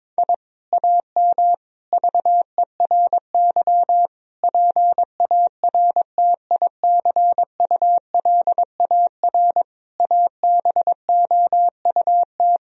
Morse code, heard as a keyed tone, about 22 wpm, 700 Hz. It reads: I AM VERY PARTICULAR ABOUT
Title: Sentences from Top 1000 Words - Review Part 4 - 22wpm